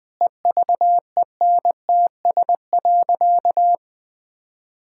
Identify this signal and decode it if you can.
Morse code, heard as a keyed tone, about 19 wpm, 700 Hz. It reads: EVENTS.
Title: Sentences with 8-Word Phrases - 20wpm